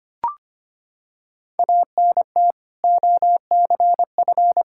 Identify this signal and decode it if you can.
Morse code, heard as a keyed tone, about 25 wpm, 700 Hz. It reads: ANT OCF
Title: Practice QSO - Exchange 2 - Info - 25wpm